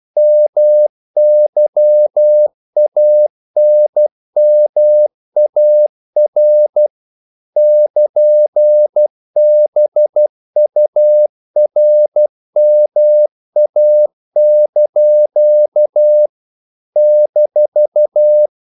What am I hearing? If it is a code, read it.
Morse code, heard as a keyed tone, about 12 wpm, 600 Hz. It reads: MYANMAR <KN>BURMA) -
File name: CW / Practice / NationalCapitals12